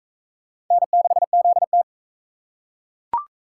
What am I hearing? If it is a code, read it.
Morse code, heard as a keyed tone, about 40 wpm, 700 Hz. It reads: N6ZT